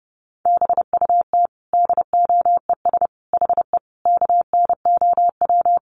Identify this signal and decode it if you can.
Morse code, heard as a keyed tone, about 30 wpm, 700 Hz. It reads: BUT DOES HE KNOW
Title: Sentences from Top 1200 Words 30wpm